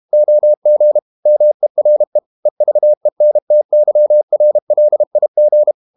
Morse code, heard as a keyed tone, about 32 wpm, 600 Hz. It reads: OG MERE EVENTYRLIG
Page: Ferie 32 wpm